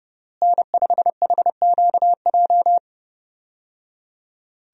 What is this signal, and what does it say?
Morse code, heard as a keyed tone, about 30 wpm, 700 Hz. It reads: N5HQJ
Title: USA Call Signs - 5 Characters - Repeated 3x - 30wpm